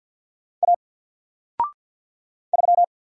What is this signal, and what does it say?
Morse code, heard as a keyed tone, about 50 wpm, 700 Hz. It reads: A 3
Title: Single Character (Letter:Number:Punctuation:Prosign) 50wpm